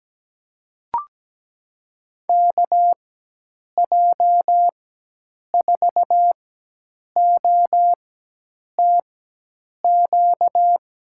Morse code, heard as a keyed tone, about 17 wpm, spaced as for 8 wpm, 700 Hz. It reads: KJ4OTQ
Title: USA Call Signs - 6 Characters - 4x Character Spacing - 17wpm